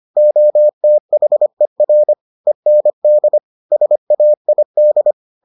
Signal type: Morse code

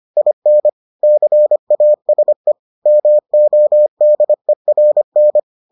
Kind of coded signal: Morse code

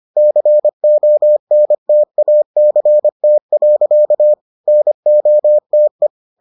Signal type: Morse code

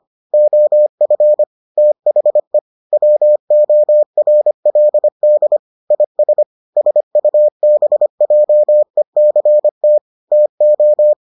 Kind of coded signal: Morse code

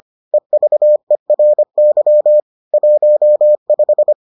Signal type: Morse code

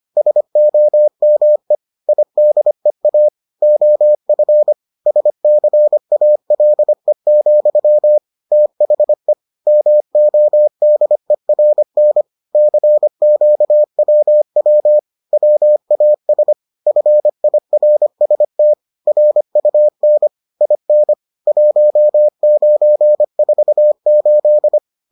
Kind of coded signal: Morse code

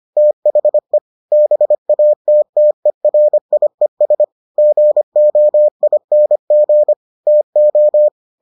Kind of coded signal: Morse code

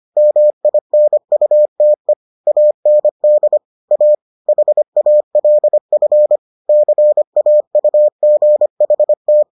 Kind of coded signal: Morse code

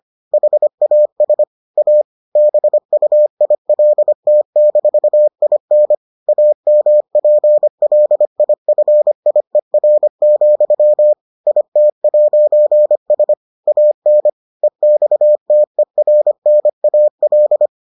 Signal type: Morse code

